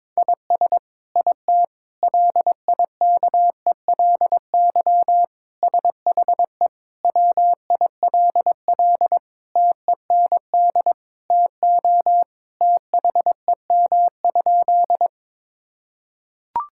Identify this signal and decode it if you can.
Morse code, heard as a keyed tone, about 22 wpm, 700 Hz. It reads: IS IT LIKELY SHE WILL TEND TO THEM?
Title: Sentences from Top 1000 Words - Review Part 5 - 22wpm